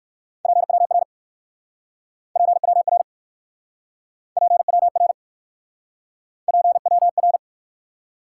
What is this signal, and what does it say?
Morse code, heard as a keyed tone, about 58 wpm, 700 Hz. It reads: PWR PWR PWR PWR